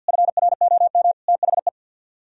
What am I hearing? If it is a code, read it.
Morse code, heard as a keyed tone, about 60 wpm, 700 Hz. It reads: FROM THE
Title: Speed-Racing - Sets of 3 Words 50wpm